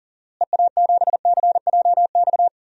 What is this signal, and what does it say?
Morse code, heard as a keyed tone, about 40 wpm, 700 Hz. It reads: EA7CJX
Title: DXCC-100 Call Signs - 6 Characters - Repeated 3x - 40wpm